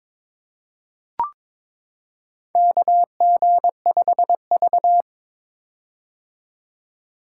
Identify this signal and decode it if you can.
Morse code, heard as a keyed tone, about 22 wpm, 700 Hz. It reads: KG5V
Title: K1USN SST QSO Practice - 22wpm